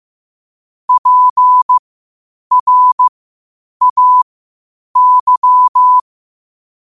Morse code, heard as a keyed tone, about 15 wpm, spaced as for 8 wpm, 1000 Hz. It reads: PRAY
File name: qotdfiles / 8wpm